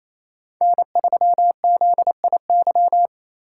Morse code, heard as a keyed tone, about 33 wpm, 700 Hz. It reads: N3ZIY
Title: Speed Racing - USA 1x3 Call Signs 28wpm